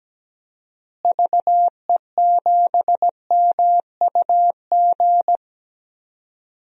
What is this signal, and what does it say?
Morse code, heard as a keyed tone, about 17 wpm, 700 Hz. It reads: VE7MUG
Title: DXCC-100 Call Signs - 6 Characters - Repeated 3x - 17wpm